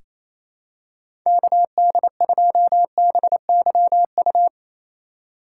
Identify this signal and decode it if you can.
Morse code, heard as a keyed tone, about 27 wpm, 700 Hz. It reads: KD2BYU
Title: USA Call Signs - 4x Character Spacing - 28wpm